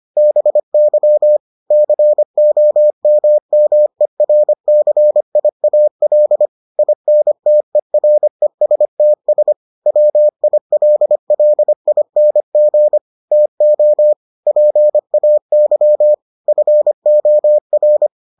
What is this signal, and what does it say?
Morse code, heard as a keyed tone, about 25 wpm, 600 Hz. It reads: BY COMMERCIAL INTERESTS WILLING TO PAY FOR